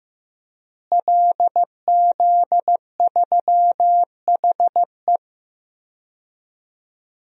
Morse code, heard as a keyed tone, about 16 wpm, 700 Hz. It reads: LZ3HE
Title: DXCC-100 Call Signs - 5 Characters - Repeated 2x - 15wpm